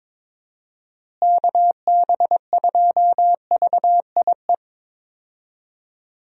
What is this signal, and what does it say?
Morse code, heard as a keyed tone, about 22 wpm, 700 Hz. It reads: KB2VIE
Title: USA Call Signs - 6 Characters - Repeated 2x - 22wpm